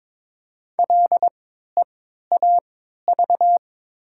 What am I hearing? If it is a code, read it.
Morse code, heard as a keyed tone, about 21 wpm, spaced as for 12 wpm, 700 Hz. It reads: LEAV
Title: 3x Character Spacing - Top 500 Words - 22wpm